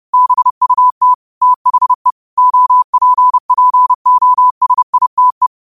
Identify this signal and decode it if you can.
Morse code, heard as a keyed tone, about 30 wpm, 1000 Hz. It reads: BUT THE OPPOSITE